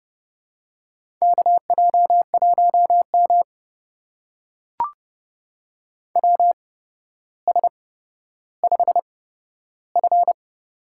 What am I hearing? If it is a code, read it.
Morse code, heard as a keyed tone, about 30 wpm, spaced as for 13 wpm, 700 Hz. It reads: KJ1M WS5F
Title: USA Call Signs - 4 Characters - 8x Character Spacing - 30wpm